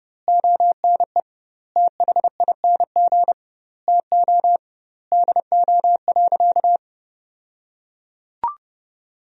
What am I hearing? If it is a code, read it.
Morse code, heard as a keyed tone, about 30 wpm, 700 Hz. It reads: ONE THING TO DO.
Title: 2x Word-Spacing - Sentences from Top 100 Words - 30wpm